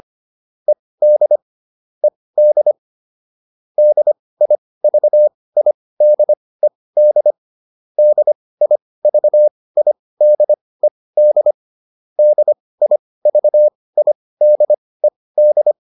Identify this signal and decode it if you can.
Morse code, heard as a keyed tone, about 25 wpm, spaced as for 18 wpm, 600 Hz. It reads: ED ED DIVIDED DIVIDED DIVIDED
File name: ED7 18